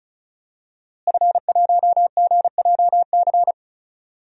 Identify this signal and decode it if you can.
Morse code, heard as a keyed tone, about 36 wpm, 700 Hz. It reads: F1GJC
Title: DXCC-100 Call Signs - 5 Characters - Repeated 3x - 35wpm